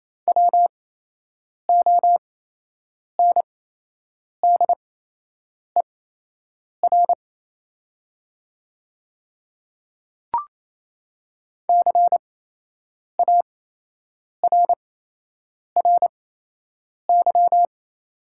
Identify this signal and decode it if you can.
Morse code, heard as a keyed tone, about 28 wpm, spaced as for 8 wpm, 700 Hz. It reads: WONDER CARRY
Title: 8x Character Spacing - Top 500 Words - 28wpm